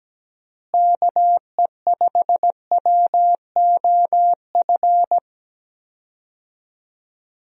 Morse code, heard as a keyed tone, about 17 wpm, 700 Hz. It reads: KE5WOF
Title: USA Call Signs - 6 Characters - Repeated 3x - 17wpm